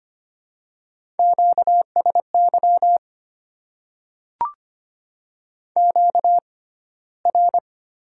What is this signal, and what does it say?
Morse code, heard as a keyed tone, about 25 wpm, 700 Hz. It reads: Q S Y Q R
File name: Top 90 QSO Elements - 6x Character Spacing - 25wpm